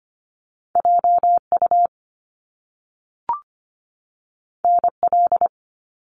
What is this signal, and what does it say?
Morse code, heard as a keyed tone, about 25 wpm, 700 Hz. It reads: JU NL
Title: Top Two Letter Combinations in Words 25wpm